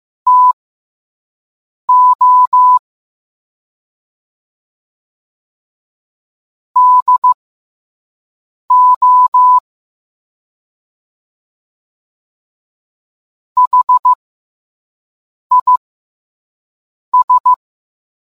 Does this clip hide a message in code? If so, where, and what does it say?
Yes: Morse code from 0.3 to 17.5 s, heard as a keyed tone, about 15 wpm, spaced as for 5 wpm, 1000 Hz. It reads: TO DO HIS